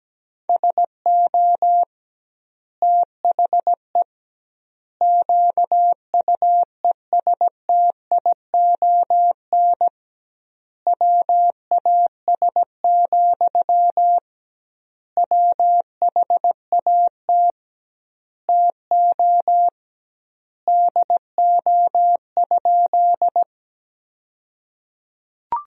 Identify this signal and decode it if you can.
Morse code, heard as a keyed tone, about 17 wpm, 700 Hz. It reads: SO THE QUESTION WAS, WHAT TO DO?
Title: 2x Word-Spacing - Sentences from Top 400 Words - 17wpm